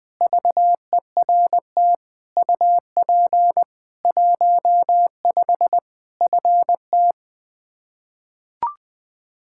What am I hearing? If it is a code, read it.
Morse code, heard as a keyed tone, about 20 wpm, 700 Hz. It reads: VERT UP 15 FT
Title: Practice QSO - Exchange 2 - Info - 20wpm